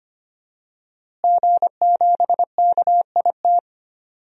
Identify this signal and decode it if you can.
Morse code, heard as a keyed tone, about 25 wpm, 700 Hz. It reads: G7KIT
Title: DXCC-100 Call Signs - 5 Characters - Repeated 2x - 25wpm